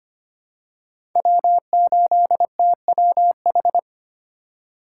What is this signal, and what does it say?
Morse code, heard as a keyed tone, about 25 wpm, 700 Hz. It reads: W8TWH
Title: USA Call Signs - 5 Characters - 2x Character Spacing - 25wpm